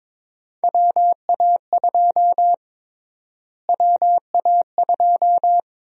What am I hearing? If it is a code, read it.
Morse code, heard as a keyed tone, about 22 wpm, 700 Hz. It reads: WA2 WA2
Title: USA Call Sign Prefixes - Repeated 2x - 22wpm